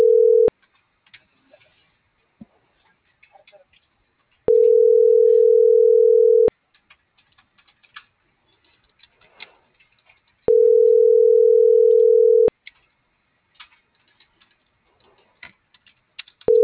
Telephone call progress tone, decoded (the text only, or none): ringback tone